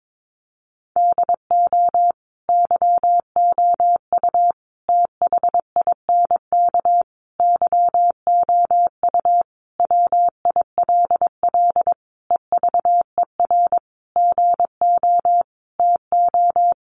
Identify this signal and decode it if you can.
Morse code, heard as a keyed tone, about 22 wpm, 700 Hz. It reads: DO YOU THINK YOU WILL EVER GO TO
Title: Sentences from Top 1300 Words 22wpm